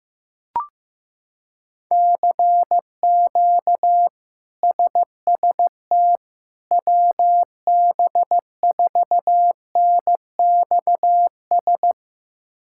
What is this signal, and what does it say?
Morse code, heard as a keyed tone, about 15 wpm, 700 Hz. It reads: CQ SST WB4NXS
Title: K1USN SST QSO Practice - 15wpm